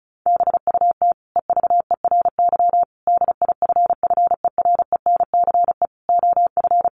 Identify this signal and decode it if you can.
Morse code, heard as a keyed tone, about 34 wpm, 700 Hz. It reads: BUT EVERY DIFFERENCE OF
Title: Sentences from Top 1400 Words 35wpm